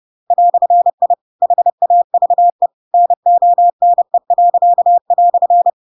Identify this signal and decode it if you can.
Morse code, heard as a keyed tone, about 30 wpm, 700 Hz. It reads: "I HAVE NONE."
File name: Pinocchio / Pinocchio ch8 30 30 ik1hge